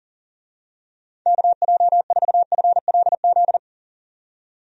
Morse code, heard as a keyed tone, about 40 wpm, 700 Hz. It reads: KJ4FLZ
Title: USA Call Signs - 6 Characters - 40wpm